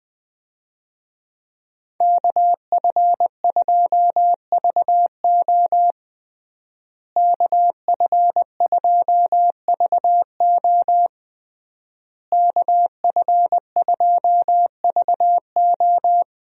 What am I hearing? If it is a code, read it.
Morse code, heard as a keyed tone, about 20 wpm, 700 Hz. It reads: KF2VO KF2VO KF2VO
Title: USA Call Signs - 5 Characters - Repeated 3x - 20wpm